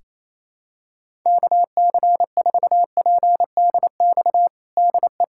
Morse code, heard as a keyed tone, about 28 wpm, 700 Hz. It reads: KC4PDX DE
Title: USA Callsign DE Callsign 28wpm